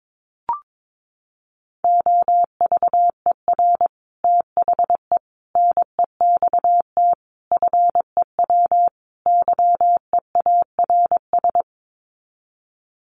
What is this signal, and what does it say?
Morse code, heard as a keyed tone, about 22 wpm, 700 Hz. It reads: OVER THE NEXT FEW YEARS
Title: Sets of 5 Words Encore 22wpm